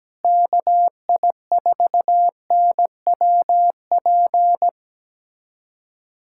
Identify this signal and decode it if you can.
Morse code, heard as a keyed tone, about 17 wpm, 700 Hz. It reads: KI4NWP
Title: USA Callsign DE Callsign 17wpm